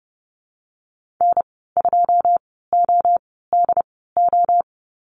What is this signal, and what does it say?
Morse code, heard as a keyed tone, about 30 wpm, spaced as for 17 wpm, 700 Hz. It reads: N2ODO
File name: Lesson 14g - Number 2 - Call Signs 30wpm